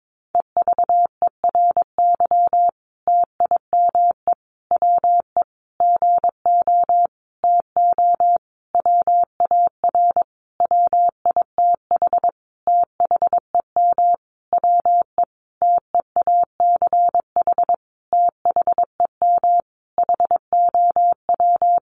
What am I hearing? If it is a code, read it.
Morse code, heard as a keyed tone, about 22 wpm, 700 Hz. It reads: EVERY TIME WE GO TO WAR WITH THEM WE TEACH THEM HOW
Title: Sentences from Top 1300 Words 22wpm